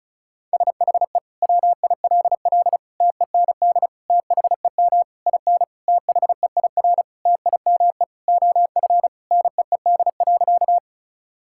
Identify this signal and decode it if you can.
Morse code, heard as a keyed tone, about 36 wpm, 700 Hz. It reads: SHE WILL TEND THEM IN THEIR TIME OF NEED.
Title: Sentences from Top 1000 Words - Review Part 5 - 35wpm